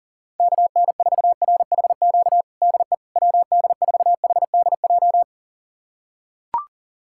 Morse code, heard as a keyed tone, about 37 wpm, 700 Hz. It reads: KN4RHQ DE WD4HDJ
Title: USA Callsign DE Callsign - 6 Characters - 40wpm